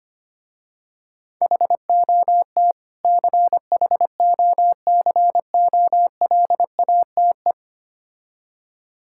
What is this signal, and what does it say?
Morse code, heard as a keyed tone, about 25 wpm, 700 Hz. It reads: HOT CHOCOLATE
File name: practice / 25wpm / Difficult Words with Hints - 25wpm